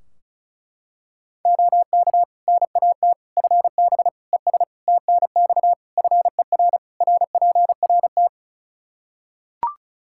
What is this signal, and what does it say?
Morse code, heard as a keyed tone, about 34 wpm, 700 Hz. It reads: OK NAT FB ES TNX FER RPRT
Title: Practice QSO - Exchange 2 - Info - 35wpm